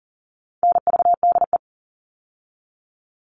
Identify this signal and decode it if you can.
Morse code, heard as a keyed tone, about 38 wpm, 700 Hz. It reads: NVDE
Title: All Cut Numbers - 4-Digit 40wpm